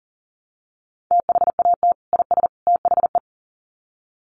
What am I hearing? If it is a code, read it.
Morse code, heard as a keyed tone, about 40 wpm, 700 Hz. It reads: THAT IS THE
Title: Sets of 3 Words 40wpm